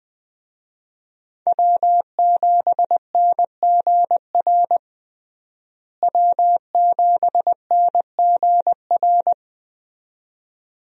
Morse code, heard as a keyed tone, about 20 wpm, 700 Hz. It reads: W7NGR W7NGR
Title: USA Call Signs - 5 Characters - Repeated 2x - 20wpm